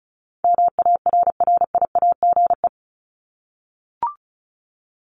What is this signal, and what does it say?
Morse code, heard as a keyed tone, about 34 wpm, 700 Hz. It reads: MARRIAGE
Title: Top 751-1000 Words 35wpm